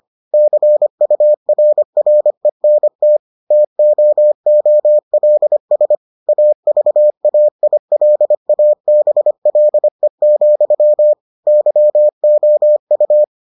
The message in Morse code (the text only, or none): CURRENT TOOLS AVAILABLE, YOU